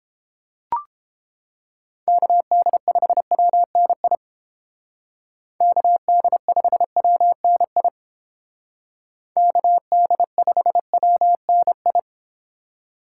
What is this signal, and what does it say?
Morse code, heard as a keyed tone, about 29 wpm, 700 Hz. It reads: KD5WNI KD5WNI KD5WNI